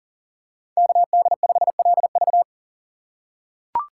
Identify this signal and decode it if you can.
Morse code, heard as a keyed tone, about 40 wpm, 700 Hz. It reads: KD5LV
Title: USA Call Signs - 5 Characters - 4x Character Spacing - 40wpm